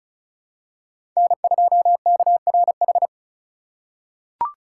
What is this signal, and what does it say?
Morse code, heard as a keyed tone, about 34 wpm, 700 Hz. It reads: N2KRH
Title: USA Call Signs - 5 Characters - 2x Character Spacing - 35wpm